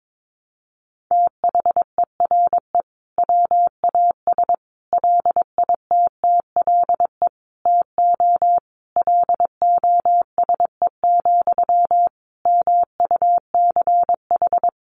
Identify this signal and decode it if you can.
Morse code, heard as a keyed tone, about 22 wpm, 700 Hz. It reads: THERE WAS LITTLE TO LOSE, MUCH
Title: Sentences from Top 1100 Words 22wpm